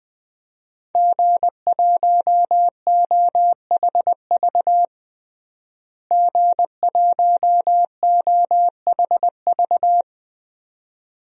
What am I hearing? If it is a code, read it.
Morse code, heard as a keyed tone, about 20 wpm, 700 Hz. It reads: G1OHV G1OHV
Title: DXCC-100 Call Signs - 5 Characters - Repeated 2x - 20wpm